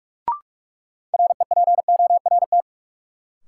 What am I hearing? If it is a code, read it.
Morse code, heard as a keyed tone, about 46 wpm, 700 Hz. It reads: REPORT
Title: Sentences from Top 900 Words - New Word Drill - 45wpm